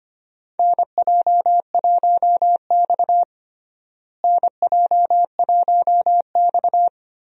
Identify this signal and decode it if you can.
Morse code, heard as a keyed tone, about 25 wpm, 700 Hz. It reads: NJ1X NJ1X
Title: USA Call Signs - 4 Characters - Repeated 3x - 25wpm